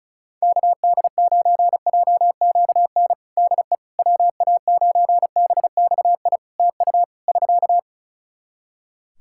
Morse code, heard as a keyed tone, about 34 wpm, 700 Hz. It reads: KD9JQN DE WA9BXI TU <SK>
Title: Practice QSO - Exchange 4 - Ending - 35wpm